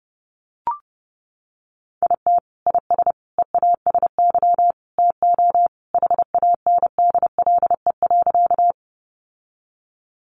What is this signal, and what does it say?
Morse code, heard as a keyed tone, about 30 wpm, 700 Hz. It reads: IT IS EASY TO HANDLE.
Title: Sentences from Top 1100 Words 30wpm